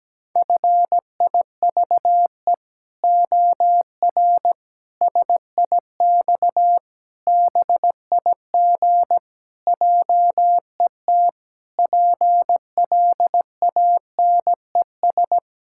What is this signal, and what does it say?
Morse code, heard as a keyed tone, about 17 wpm, 700 Hz. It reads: FIVE OR SIX BIG JET PLANES
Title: Pangram Challenge v2 - 17wpm